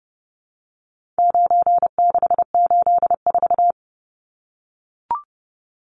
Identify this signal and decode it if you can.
Morse code, heard as a keyed tone, about 30 wpm, 700 Hz. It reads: 9684
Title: Numbers - 4-Digit 30wpm